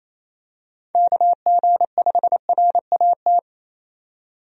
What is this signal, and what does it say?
Morse code, heard as a keyed tone, about 28 wpm, 700 Hz. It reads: KG5RAT
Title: USA Call Signs - 6 Characters - 28wpm